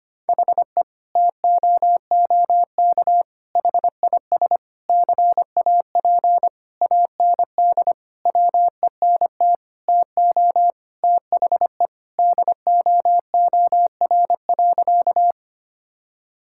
Morse code, heard as a keyed tone, about 25 wpm, 700 Hz. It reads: HE TOOK HIS CAP AND WENT TO THE DOOR.
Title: Sentences from Top 1600 Words 25wpm